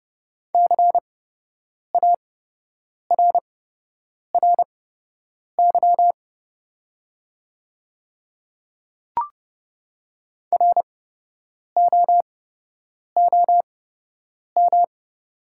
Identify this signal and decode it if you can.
Morse code, heard as a keyed tone, about 30 wpm, spaced as for 8 wpm, 700 Hz. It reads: CARRY ROOM
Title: 8x Character Spacing - Top 300 Words - 30wpm